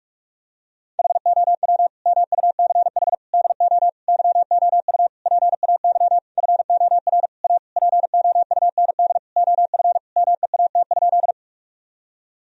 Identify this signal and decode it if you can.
Morse code, heard as a keyed tone, about 46 wpm, 700 Hz. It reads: HOW MUCH DO YOU PAY FOR A POUND OF MEAT?
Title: Sentences from Top 1000 Words - Review Part 3 - 45wpm